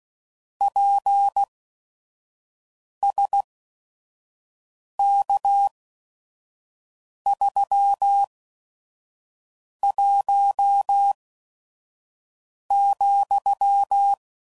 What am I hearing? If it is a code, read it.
Morse code, heard as a keyed tone, about 16 wpm, spaced as for 5 wpm, 800 Hz. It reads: PSK31,
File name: CW Code Groups @5WPM / QSOs / kc7sac